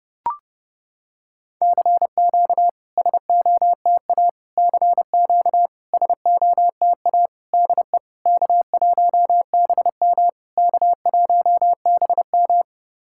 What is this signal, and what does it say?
Morse code, heard as a keyed tone, about 30 wpm, 700 Hz. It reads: CQ SOTA CQ SOTA DE K1BM K1BM
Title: SOTA QSO Practice 30wpm